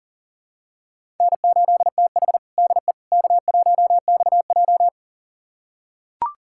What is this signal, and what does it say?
Morse code, heard as a keyed tone, about 37 wpm, 700 Hz. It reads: N8TH DE K1XJ
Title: USA Callsign DE Callsign - 4 Characters - 40wpm